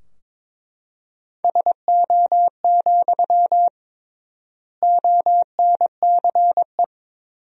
Morse code, heard as a keyed tone, about 22 wpm, 700 Hz. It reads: SO, ONCE
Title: 3x Word-Spacing - Sentences from Top 300 Words - 22wpm